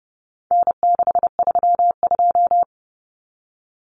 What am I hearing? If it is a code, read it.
Morse code, heard as a keyed tone, about 30 wpm, 700 Hz. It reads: N632
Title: Common Cut Numbers - 4-Digit 30wpm